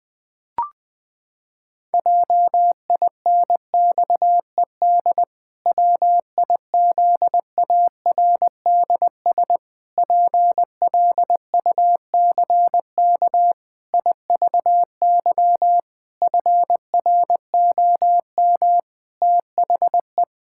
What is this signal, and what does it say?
Morse code, heard as a keyed tone, about 21 wpm, 700 Hz. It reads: JINXED WIZARDS PLUCK IVY FROM THE
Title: Pangram Challenge v2 - 20wpm